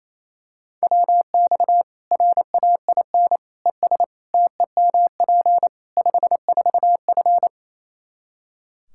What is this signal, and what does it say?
Morse code, heard as a keyed tone, about 28 wpm, 700 Hz. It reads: WX RAIN ES TEMP 54F
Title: Practice QSO - Exchange 2 - Info - 28wpm